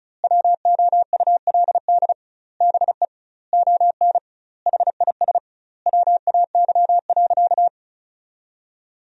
Morse code, heard as a keyed tone, about 34 wpm, 700 Hz. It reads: WOULD BE ON HIS WAY.
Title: 2x Word-Spacing - Sentences from Top 300 Words - 35wpm